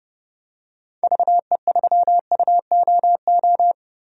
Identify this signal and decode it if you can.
Morse code, heard as a keyed tone, about 30 wpm, 700 Hz. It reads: VE3UOO
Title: DXCC-100 Call Signs - 6 Characters - Repeated 3x - 30wpm